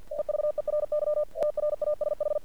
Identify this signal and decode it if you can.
Morse code, heard as a keyed tone, about 48 wpm, 600 Hz. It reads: EVERY TRASH
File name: archived / CRSnet